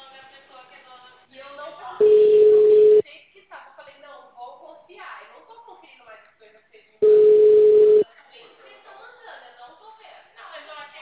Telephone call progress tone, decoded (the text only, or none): ringback tone